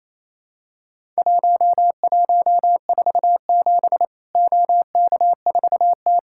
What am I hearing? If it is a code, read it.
Morse code, heard as a keyed tone, about 27 wpm, 700 Hz. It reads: 1147 OK4T
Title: Worked All Europe DX Contest - QTC Exchange - 28wpm